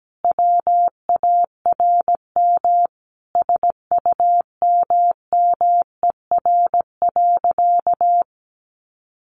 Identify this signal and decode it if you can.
Morse code, heard as a keyed tone, about 17 wpm, 700 Hz. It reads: WARM SUMMER.
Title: Sentences from Top 1100 Words 17wpm